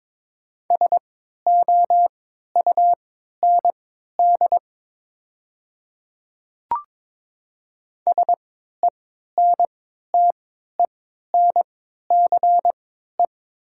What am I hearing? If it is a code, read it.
Morse code, heard as a keyed tone, about 22 wpm, spaced as for 13 wpm, 690 Hz. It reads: SOUND SENTENCE
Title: 3x Character Spacing - Top 300 Words - 22wpm